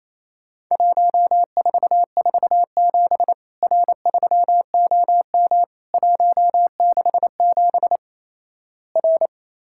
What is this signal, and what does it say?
Morse code, heard as a keyed tone, about 28 wpm, 700 Hz. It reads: 1447 R3OM 167 R
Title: Worked All Europe DX Contest - QTC Exchange - 28wpm